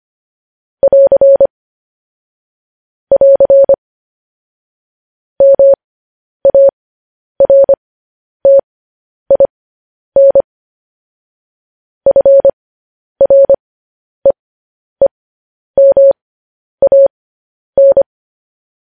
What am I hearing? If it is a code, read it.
Morse code, heard as a keyed tone, about 25 wpm, spaced as for 10 wpm, 550 Hz. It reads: <AR> <AR> MARTIN FREEMAN